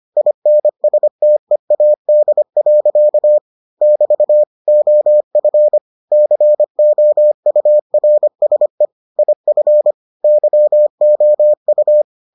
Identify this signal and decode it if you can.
Morse code, heard as a keyed tone, about 25 wpm, 600 Hz. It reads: INSTEAD. <BT> OF COURSE IF YOU